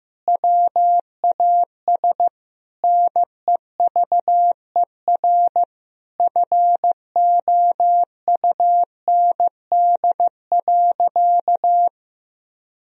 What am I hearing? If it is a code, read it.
Morse code, heard as a keyed tone, about 15 wpm, 700 Hz. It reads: WAS NEVER FOUND.
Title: Sentences from Top 1000 Words - Review Part 4 - 15wpm